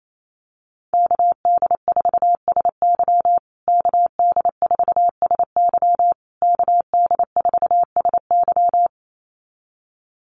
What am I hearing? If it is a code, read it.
Morse code, heard as a keyed tone, about 28 wpm, 700 Hz. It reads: KD4SY KD4SY KD4SY
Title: USA Call Signs Repeated 3x 28wpm